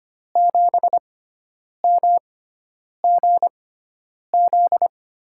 Morse code, heard as a keyed tone, about 26 wpm, spaced as for 9 wpm, 700 Hz. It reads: 7MGZ
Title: USA Call Signs - 6x Character Spacing - 25wpm